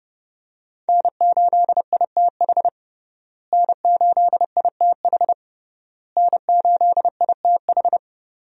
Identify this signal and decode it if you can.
Morse code, heard as a keyed tone, about 32 wpm, 700 Hz. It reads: N8ITH N8ITH N8ITH
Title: USA Call Signs - 5 Characters - Repeated 3x - 30wpm